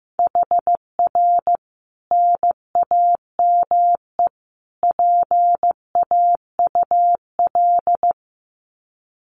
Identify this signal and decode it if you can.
Morse code, heard as a keyed tone, about 15 wpm, 700 Hz. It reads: HR NAME PAUL
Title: Common QSO Phrases 15wpm